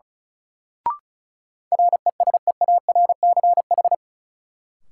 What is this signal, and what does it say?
Morse code, heard as a keyed tone, about 36 wpm, 700 Hz. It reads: RESEARCH
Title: Sentences from Top 1000 Words - Review Part 4 - New Word Drill - 35wpm